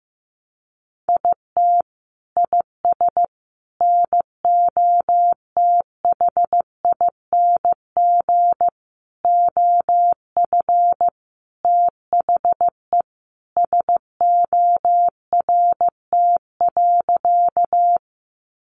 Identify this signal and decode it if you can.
Morse code, heard as a keyed tone, about 15 wpm, 700 Hz. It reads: IT IS NOTHING OF THE SORT.
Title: Sentences from Top 1300 Words 15wpm